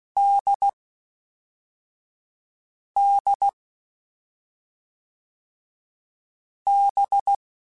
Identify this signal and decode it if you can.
Morse code, heard as a keyed tone, about 16 wpm, 800 Hz. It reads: D D B
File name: Difficult Combos / Difficult Combos-mp3s / NDB NDB6 @5